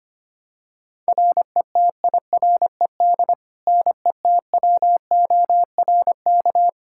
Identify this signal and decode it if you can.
Morse code, heard as a keyed tone, about 25 wpm, 700 Hz. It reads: RETIRED NETWORK
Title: Practice QSO - Exchange 3 - More Info - 25wpm